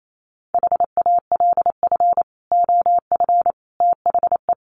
Morse code, heard as a keyed tone, about 28 wpm, 700 Hz. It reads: HALF OF THE
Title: Sets of 3 Words 28wpm